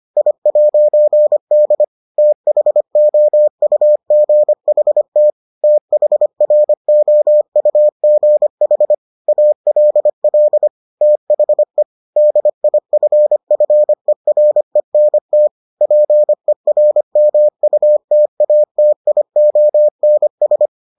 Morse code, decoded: I'D THOUGHT THROUGH ALL THE DIFFERENT PERMUTATIONS

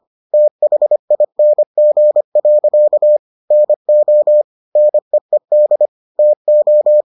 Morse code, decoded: THING. NO NEED TO

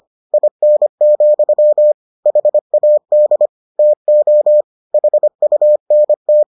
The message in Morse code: IN, HAD TO HUNT

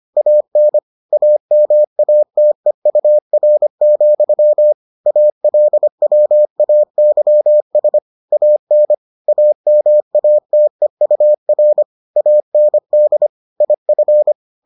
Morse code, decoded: AN AMATEUR, ALWAYS AN AMATEUR AND IF

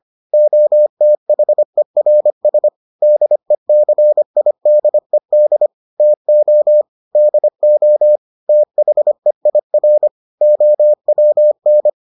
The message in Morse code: OTHERS DECIDED TO DO THEIR OWN

